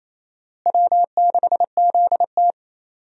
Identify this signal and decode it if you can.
Morse code, heard as a keyed tone, about 28 wpm, 700 Hz. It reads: W6ZT
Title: USA Call Signs - 4 Characters - Repeated 2x - 28wpm